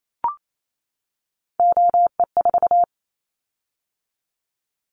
Morse code, heard as a keyed tone, about 28 wpm, 700 Hz. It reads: OE4
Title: Callsign Prefixes for Top 25 DXCC - 28wpm